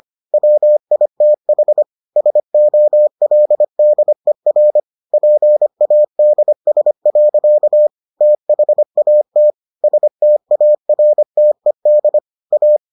Morse code, heard as a keyed tone, about 25 wpm, 600 Hz. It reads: WITH SOLDER PADS. THAT STARTED A